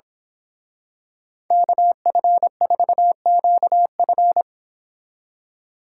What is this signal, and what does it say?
Morse code, heard as a keyed tone, about 27 wpm, 700 Hz. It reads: KF4QF
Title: Speed Racing - USA 2x2 Call Signs 22wpm